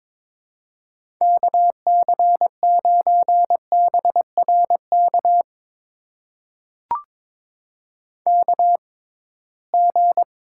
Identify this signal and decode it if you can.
Morse code, heard as a keyed tone, about 21 wpm, spaced as for 16 wpm, 700 Hz. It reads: KC9BRK KG
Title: USA Call Signs - 6 Characters - 6x Character Spacing - 22wpm